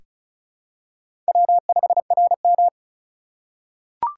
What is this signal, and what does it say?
Morse code, heard as a keyed tone, about 36 wpm, 690 Hz. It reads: W5RM
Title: USA Call Signs - 4 Characters - 2x Character Spacing - 35wpm